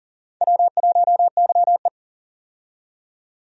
Morse code, heard as a keyed tone, about 37 wpm, 700 Hz. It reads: W1YE
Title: USA Call Signs - 4 Characters - 40wpm